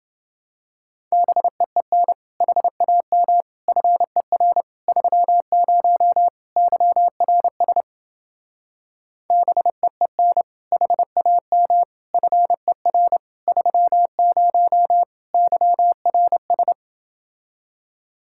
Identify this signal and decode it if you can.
Morse code, heard as a keyed tone, about 29 wpm, 700 Hz. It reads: BEEN HAM FER 30 YRS BEEN HAM FER 30 YRS